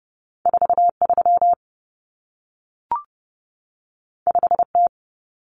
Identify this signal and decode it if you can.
Morse code, heard as a keyed tone, about 30 wpm, 700 Hz. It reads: 43 5T
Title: Common Cut Numbers - 2-Digit 30wpm